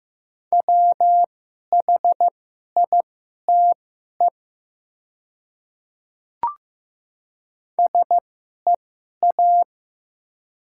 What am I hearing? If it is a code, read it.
Morse code, heard as a keyed tone, about 15 wpm, spaced as for 11 wpm, 700 Hz. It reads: WHITE SEA
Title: 2x Character Spacing - Top 300 Words - 15wpm